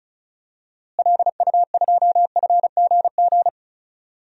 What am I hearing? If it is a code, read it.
Morse code, heard as a keyed tone, about 36 wpm, 700 Hz. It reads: LU2FGG
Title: DXCC-100 Call Signs - 6 Characters - Repeated 2x - 35wpm